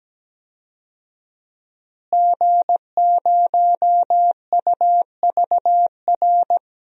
Morse code, heard as a keyed tone, about 17 wpm, 700 Hz. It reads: G0UVR
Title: DXCC-100 Call Signs - 5 Characters - Repeated 3x - 17wpm